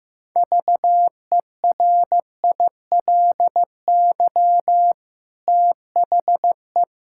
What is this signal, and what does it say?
Morse code, heard as a keyed tone, about 15 wpm, 700 Hz. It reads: VERILY THE
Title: Pangram Challenge v2 - 15wpm